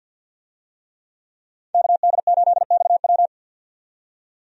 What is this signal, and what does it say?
Morse code, heard as a keyed tone, about 53 wpm, 700 Hz. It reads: KD8XW
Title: USA Call Signs - 5 Characters - 50wpm